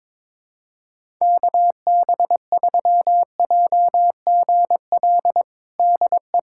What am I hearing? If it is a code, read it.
Morse code, heard as a keyed tone, about 22 wpm, 700 Hz. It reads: KB3JGL DE
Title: USA Callsign DE Callsign 22wpm